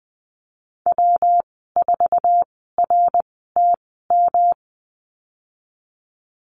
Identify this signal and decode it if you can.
Morse code, heard as a keyed tone, about 20 wpm, spaced as for 15 wpm, 700 Hz. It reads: W4RTM
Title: 154 Lesson 25g - Letter p - Call Signs 20wpm